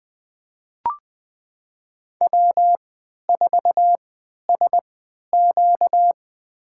Morse code, heard as a keyed tone, about 21 wpm, spaced as for 11 wpm, 700 Hz. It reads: W4SQ
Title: USA Call Signs - 4 Characters - 3x Character Spacing - 20wpm